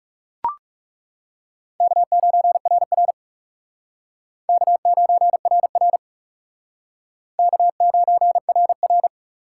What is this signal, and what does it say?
Morse code, heard as a keyed tone, about 37 wpm, 700 Hz. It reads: K9RR K9RR K9RR
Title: Speed Racing - USA Call Signs 30wpm